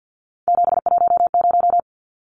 Morse code, heard as a keyed tone, about 50 wpm, 700 Hz. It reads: 710
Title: Numbers - 3-Digit 50wpm